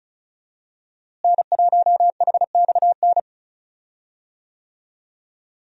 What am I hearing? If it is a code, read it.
Morse code, heard as a keyed tone, about 36 wpm, 700 Hz. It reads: N1HXN
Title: USA Call Signs - 5 Characters - Repeated 3x - 35wpm